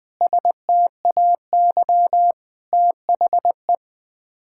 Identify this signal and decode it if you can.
Morse code, heard as a keyed tone, about 20 wpm, 700 Hz. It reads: STAY THE
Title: Sets of 9 Words - 20wpm